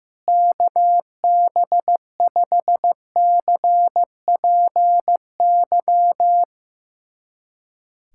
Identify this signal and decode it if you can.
Morse code, heard as a keyed tone, about 15 wpm, 700 Hz. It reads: KB5CPY
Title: USA Callsign DE Callsign 15wpm